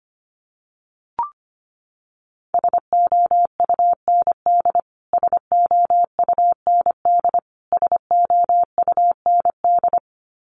Morse code, heard as a keyed tone, about 25 wpm, 700 Hz. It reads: SOUND SOUND SOUND
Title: Top 500 Words Repeated 25wpm